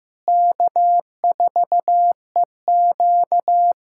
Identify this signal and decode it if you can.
Morse code, heard as a keyed tone, about 15 wpm, 700 Hz. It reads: K4EQ
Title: USA Callsign DE Callsign - 4 Characters - 15wpm